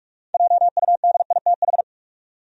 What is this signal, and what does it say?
Morse code, heard as a keyed tone, about 42 wpm, 700 Hz. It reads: JUDITH
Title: USA Common Names - 45wpm